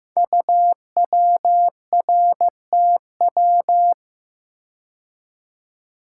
Morse code, heard as a keyed tone, about 15 wpm, 700 Hz. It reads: UWRTW
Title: Code Groups - 5 Letters - 15wpm